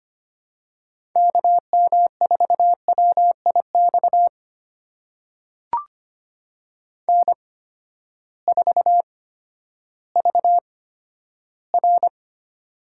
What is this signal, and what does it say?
Morse code, heard as a keyed tone, about 25 wpm, spaced as for 12 wpm, 700 Hz. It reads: KM4WIX N4VR
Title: USA Call Signs - 8x Character Spacing - 25wpm